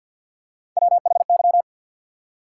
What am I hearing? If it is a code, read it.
Morse code, heard as a keyed tone, about 54 wpm, 700 Hz. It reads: WHY
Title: Three Letter Words - 50wpm